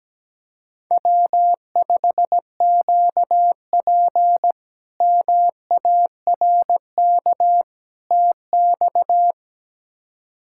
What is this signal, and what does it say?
Morse code, heard as a keyed tone, about 17 wpm, 700 Hz. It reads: W5QP MARK TX
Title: K1USN SST QSO Practice - 17wpm